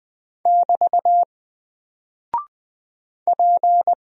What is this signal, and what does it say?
Morse code, heard as a keyed tone, about 20 wpm, 700 Hz. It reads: <BT> P
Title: Single Character (Letter:Number:Punctuation:Prosign) Alt 20wpm